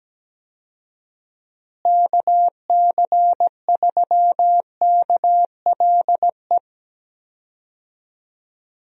Morse code, heard as a keyed tone, about 17 wpm, 700 Hz. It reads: KC3KLE
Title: USA Call Signs - 6 Characters - Repeated 3x - 17wpm